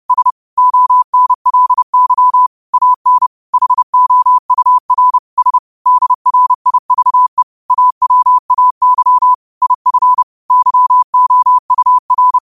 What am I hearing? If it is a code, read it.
Morse code, heard as a keyed tone, about 30 wpm, 1000 Hz. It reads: S ONLY AN HOUR S DRIVE AWAY IF YOUR